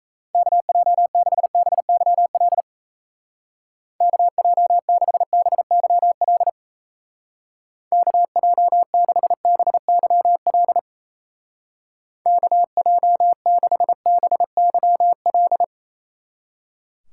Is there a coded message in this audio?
Yes: Morse code from 0.3 to 15.6 s, heard as a keyed tone, about 37 wpm, 700 Hz. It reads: KJ6BYL KJ6BYL KJ6BYL KJ6BYL